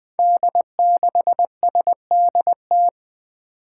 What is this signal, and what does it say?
Morse code, heard as a keyed tone, about 21 wpm, 700 Hz. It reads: D6SDT
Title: Five Challenging Letter-Number Combinations 20wpm